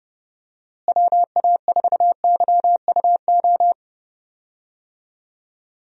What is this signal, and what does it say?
Morse code, heard as a keyed tone, about 30 wpm, 700 Hz. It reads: WA4YUO
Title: USA Call Signs - 6 Characters - Repeated 2x - 30wpm